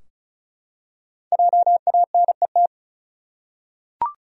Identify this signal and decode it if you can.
Morse code, heard as a keyed tone, about 36 wpm, 700 Hz. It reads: JANET
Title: USA Common Names - Max 5 Letters - 35wpm